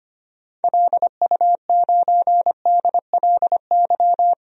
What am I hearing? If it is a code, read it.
Morse code, heard as a keyed tone, about 25 wpm, 690 Hz. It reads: LU9DLY
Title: DXCC-100 Call Signs - 6 Characters - 25wpm